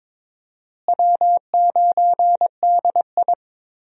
Morse code, heard as a keyed tone, about 22 wpm, 700 Hz. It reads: W9DI
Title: USA Call Signs - 4 Characters - Repeated 3x - 22wpm